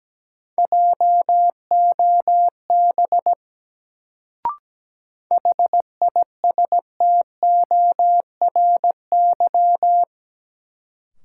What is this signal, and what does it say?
Morse code, heard as a keyed tone, about 17 wpm, 700 Hz. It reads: JOB HISTORY
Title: Sentences from Top 600 Words - New Word Drill - 17wpm